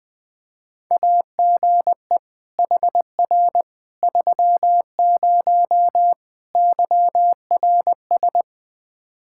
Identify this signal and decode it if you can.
Morse code, heard as a keyed tone, about 20 wpm, 700 Hz. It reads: AGE HR 30 YRS
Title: Practice QSO - Exchange 3 - More Info - 20wpm